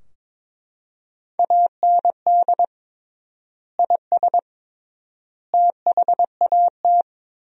Morse code, heard as a keyed tone, about 22 wpm, 700 Hz. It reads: AND IS THAT
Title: 3x Word-Spacing - Sentences from Top 100 Words - 22wpm